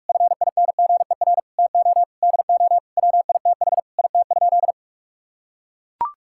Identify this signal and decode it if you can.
Morse code, heard as a keyed tone, about 41 wpm, 700 Hz. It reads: FINGER TO DO WITH IT?
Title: Sentences from Top 1600 Words 45wpm